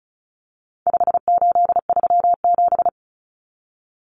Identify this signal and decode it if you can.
Morse code, heard as a keyed tone, about 34 wpm, 700 Hz. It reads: 5837
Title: Numbers - 4-Digit 35wpm